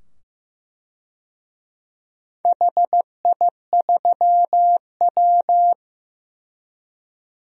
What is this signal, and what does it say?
Morse code, heard as a keyed tone, about 15 wpm, 700 Hz. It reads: HI3W
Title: DXCC-100 Call Signs - 4 Characters - 15wpm